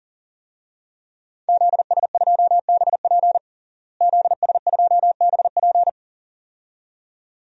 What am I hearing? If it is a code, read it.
Morse code, heard as a keyed tone, about 37 wpm, 700 Hz. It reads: ZS2BP ZS2BP
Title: DXCC-100 Call Signs - 5 Characters - Repeated 2x - 40wpm